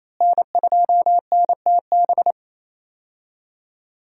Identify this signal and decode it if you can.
Morse code, heard as a keyed tone, about 28 wpm, 700 Hz. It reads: N2NTB
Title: USA Call Signs - 5 Characters - Repeated 3x - 28wpm